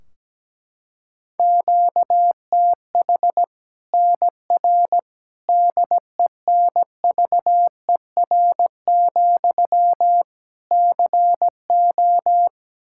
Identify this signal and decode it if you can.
Morse code, heard as a keyed tone, about 17 wpm, 700 Hz. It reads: QTH NR DENVER, CO
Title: Practice QSO - Exchange 1 - Beginning - 17wpm